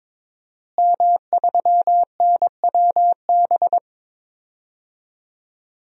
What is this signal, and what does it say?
Morse code, heard as a keyed tone, about 22 wpm, 700 Hz. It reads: M3NWB
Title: DXCC-100 Call Signs - 5 Characters - Repeated 3x - 22wpm